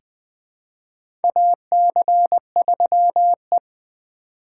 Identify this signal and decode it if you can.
Morse code, heard as a keyed tone, about 20 wpm, 700 Hz. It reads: AC3E
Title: USA Call Signs - 4 Characters - 3x Character Spacing - 20wpm